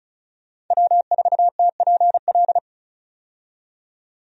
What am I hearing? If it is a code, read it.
Morse code, heard as a keyed tone, about 36 wpm, 700 Hz. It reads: W4TPL
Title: USA Call Signs - 5 Characters - 35wpm